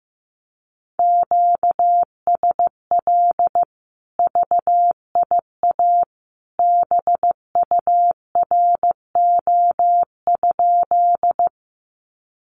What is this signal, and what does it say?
Morse code, heard as a keyed tone, about 15 wpm, 700 Hz. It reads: QSL VIA BURO?
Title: Common QSO Phrases 15wpm